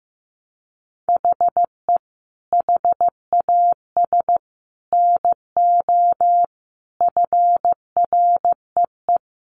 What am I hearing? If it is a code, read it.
Morse code, heard as a keyed tone, about 15 wpm, 700 Hz. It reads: HE HAS NO FREE
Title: Sentences from Top 1400 Words 15wpm